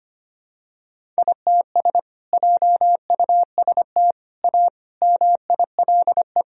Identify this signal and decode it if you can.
Morse code, heard as a keyed tone, about 25 wpm, 700 Hz. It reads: ITS JUST A MILE
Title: Sentences from Top 1000 Words - Review Part 1 - 25wpm